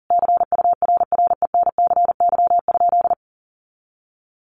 Morse code, heard as a keyed tone, about 40 wpm, 700 Hz. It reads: CURRENCY?
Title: Sentences from Top 1500 Words 40wpm